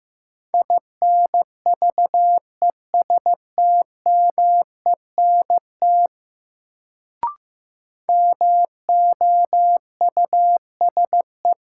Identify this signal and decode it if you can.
Morse code, heard as a keyed tone, about 15 wpm, 700 Hz. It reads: INVESTMENT MOUSE
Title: Sentences from Top 700 Words - New Word Drill - 15wpm